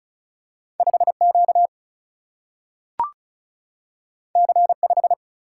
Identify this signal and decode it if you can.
Morse code, heard as a keyed tone, about 36 wpm, 700 Hz. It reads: 5Q C5
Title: Single-Letter Single-Number 35wpm